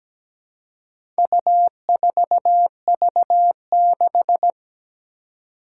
Morse code, heard as a keyed tone, about 17 wpm, 700 Hz. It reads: U4V6
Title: Four Challenging Letter-Number Combinations 17wpm